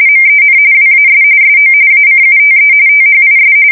RTTY, about 45.45 baud. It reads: MEN WAS SO ACUTE THIS
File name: fsk170 / usnavy2